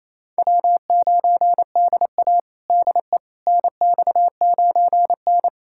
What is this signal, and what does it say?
Morse code, heard as a keyed tone, about 28 wpm, 700 Hz. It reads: W9DA DE NX9N
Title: USA Callsign DE Callsign - 4 Characters - 28wpm